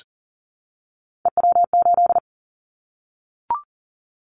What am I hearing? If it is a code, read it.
Morse code, heard as a keyed tone, about 43 wpm, 700 Hz. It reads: EW8
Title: Callsign Prefixes for Top 50 DXCC - 40wpm